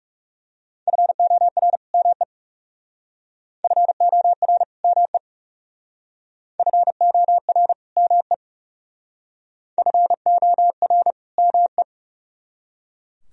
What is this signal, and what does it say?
Morse code, heard as a keyed tone, about 37 wpm, 700 Hz. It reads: FOR ME FOR ME FOR ME FOR ME
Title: Speed-Racing - Sets of 2 Words 30wpm